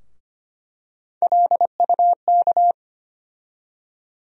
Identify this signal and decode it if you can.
Morse code, heard as a keyed tone, about 25 wpm, 700 Hz. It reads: LUK
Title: USA Call Sign Postfixes - 3 Letters 25wpm